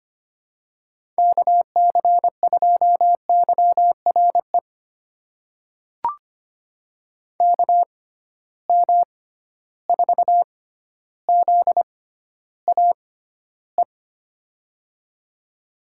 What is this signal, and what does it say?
Morse code, heard as a keyed tone, about 25 wpm, spaced as for 13 wpm, 700 Hz. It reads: KC2YRE KM4ZAE
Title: USA Call Signs - 6 Characters - 6x Character Spacing - 25wpm